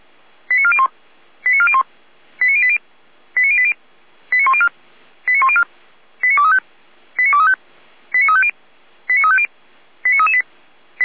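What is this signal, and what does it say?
Five-tone selective call (ZVEI), heard as quick runs of five tones, tones of about 70 ms. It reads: 89401 89401 89080 89080 89104 89104 89235 89235 89350 89350 89338
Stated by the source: ZVEI five-tone call received on an SDR.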